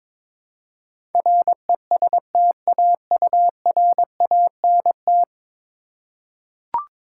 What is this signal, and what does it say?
Morse code, heard as a keyed tone, about 22 wpm, 700 Hz. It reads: RESTAURANT
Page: morse-code-ninja-practice-22wpm